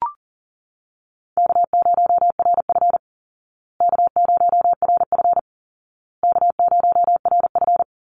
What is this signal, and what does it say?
Morse code, heard as a keyed tone, about 40 wpm, 700 Hz. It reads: K0RF K0RF K0RF
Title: Easy Call Signs Repeated 3x - 40wpm